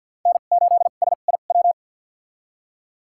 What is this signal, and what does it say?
Morse code, heard as a keyed tone, about 53 wpm, spaced as for 37 wpm, 700 Hz. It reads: N8SIW
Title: USA Call Signs - 5 Characters - 2x Character Spacing - 50wpm